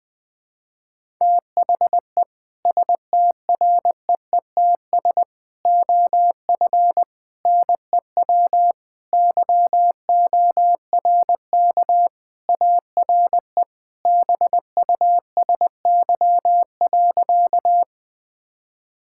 Morse code, heard as a keyed tone, about 20 wpm, 700 Hz. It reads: THE STREETS OF NEW YORK ARE BUSY.
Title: Sentences with 3-Word Phrases - Spoken Beginning - 20wpm